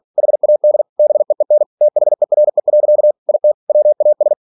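Morse code, heard as a keyed tone, about 46 wpm, 600 Hz. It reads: HAD BEEN THERE. IT WAS
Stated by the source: Practice text from an English novel sent at 47 WPM.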